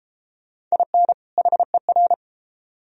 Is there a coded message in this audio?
Yes: Morse code from 0.7 to 2.1 s, heard as a keyed tone, about 32 wpm, 700 Hz. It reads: IN HER